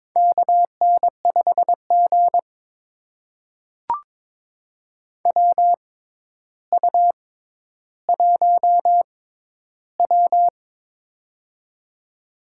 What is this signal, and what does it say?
Morse code, heard as a keyed tone, about 22 wpm, spaced as for 11 wpm, 700 Hz. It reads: KN5G WU1W
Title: USA Call Signs - 4 Characters - 6x Character Spacing - 22wpm